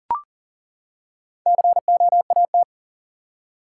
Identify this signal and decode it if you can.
Morse code, heard as a keyed tone, about 40 wpm, 700 Hz. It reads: COAT
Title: Four Letter Words - 40wpm